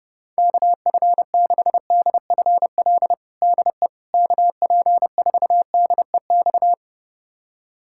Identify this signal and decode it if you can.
Morse code, heard as a keyed tone, about 30 wpm, 700 Hz. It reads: KF6DFL DE KP4DEX
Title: USA Callsign DE Callsign - 6 Characters - 30wpm